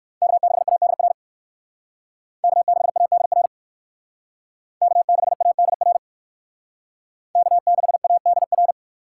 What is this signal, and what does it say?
Morse code, heard as a keyed tone, about 58 wpm, 700 Hz. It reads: K6ADR K6ADR K6ADR K6ADR